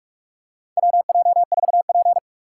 Morse code, heard as a keyed tone, about 42 wpm, 700 Hz. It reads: WJ4P
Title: USA Call Signs - 4 Characters - 2x Character Spacing - 45wpm